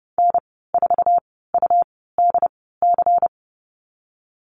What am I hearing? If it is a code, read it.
Morse code, heard as a keyed tone, about 32 wpm, spaced as for 17 wpm, 700 Hz. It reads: N4UDC
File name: Lesson 19g - Letter w - Call Signs 30wpm